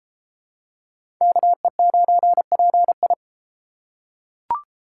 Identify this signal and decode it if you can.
Morse code, heard as a keyed tone, about 30 wpm, 700 Hz. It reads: KE9PI